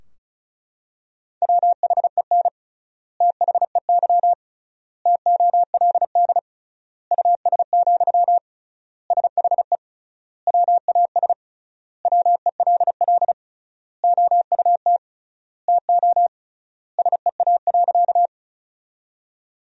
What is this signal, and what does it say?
Morse code, heard as a keyed tone, about 34 wpm, 700 Hz. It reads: WHEN THEY TOLD US, SHE WAS WELL OUT TO SEA.
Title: 3x Word-Spacing - Sentences from Top 400 Words - 35wpm